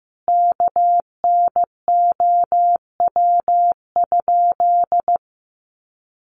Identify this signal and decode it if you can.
Morse code, heard as a keyed tone, about 15 wpm, 700 Hz. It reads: KNOW?
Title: Sentences from Top 100 Words 15wpm